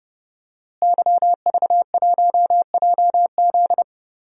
Morse code, heard as a keyed tone, about 29 wpm, 700 Hz. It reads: YV1JZ
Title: DXCC-100 Call Signs - 5 Characters - Repeated 3x - 30wpm